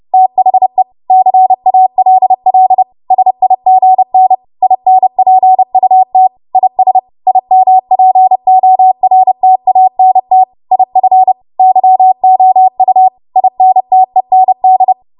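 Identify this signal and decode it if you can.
Morse code, heard as a keyed tone, about 30 wpm, 750 Hz. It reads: THE CALL SIGN INPUT IS IMPORTANT IF YOU INTEND
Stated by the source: Morse practice transmission at 30 WPM.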